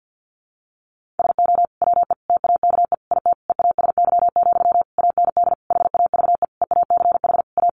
Morse code, heard as a keyed tone, about 50 wpm, 700 Hz. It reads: SO WE TAKE IT EASY, AND HAVE EACH A
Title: Sentences from Top 700 Words 50wpm